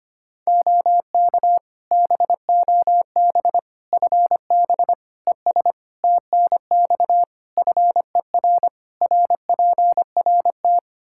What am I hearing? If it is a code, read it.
Morse code, heard as a keyed tone, about 25 wpm, 700 Hz. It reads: OK BOB FB ES TNX FER RPRT
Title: Practice QSO - Exchange 2 - Info - 25wpm